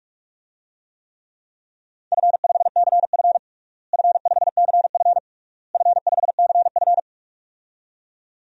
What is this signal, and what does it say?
Morse code, heard as a keyed tone, about 46 wpm, 700 Hz. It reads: F5CF F5CF F5CF
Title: DXCC-100 Call Signs - 4 Characters - Repeated 3x - 45wpm